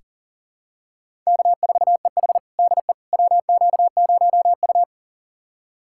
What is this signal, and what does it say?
Morse code, heard as a keyed tone, about 37 wpm, 700 Hz. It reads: K4EH DE WQ0U
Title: USA Callsign DE Callsign - 4 Characters - 40wpm